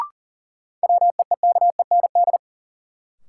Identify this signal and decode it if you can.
Morse code, heard as a keyed tone, about 40 wpm, 700 Hz. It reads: WEEKEND
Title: Sets of 5 Words - New Word Drill - 40wpm